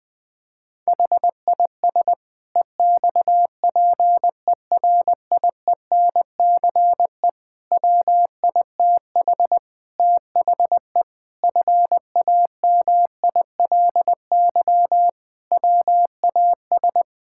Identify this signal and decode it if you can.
Morse code, heard as a keyed tone, about 21 wpm, 700 Hz. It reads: HIS EXPERIENCE WITH THE FAMILY WAS
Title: Sentences from Top 1000 Words - Review Part 6 - 20wpm